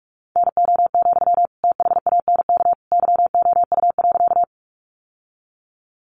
Morse code, heard as a keyed tone, about 44 wpm, 700 Hz. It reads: NO, THANK YOU.
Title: Sentences from Top 600 Words 45wpm